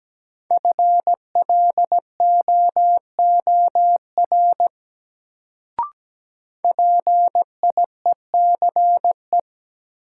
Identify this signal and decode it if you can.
Morse code, heard as a keyed tone, about 17 wpm, 700 Hz. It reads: FLOOR PIECE
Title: Sets of 3 Words - New Word Drill - 17wpm